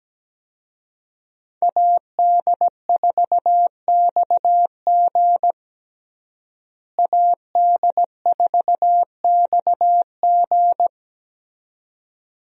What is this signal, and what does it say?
Morse code, heard as a keyed tone, about 17 wpm, 700 Hz. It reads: AD4XG AD4XG
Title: USA Call Signs - 5 Characters - Repeated 2x - 17wpm